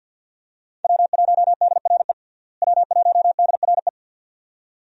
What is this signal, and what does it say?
Morse code, heard as a keyed tone, about 53 wpm, 700 Hz. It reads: W1DRE W1DRE
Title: USA Call Signs - 5 Characters - Repeated 2x - 50wpm